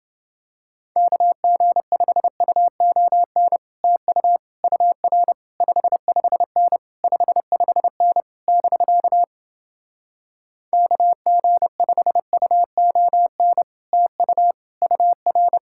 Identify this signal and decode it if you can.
Morse code, heard as a keyed tone, about 30 wpm, 700 Hz. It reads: KG5UON TU UR 55N 55N <BK> KG5UON TU UR